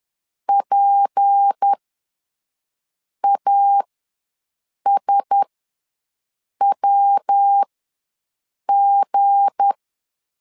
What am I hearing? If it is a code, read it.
Morse code, heard as a keyed tone, about 10 wpm, spaced as for 5 wpm, 800 Hz. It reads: PASWG
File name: LEZIONI / 9 2 ws 10wpm 4130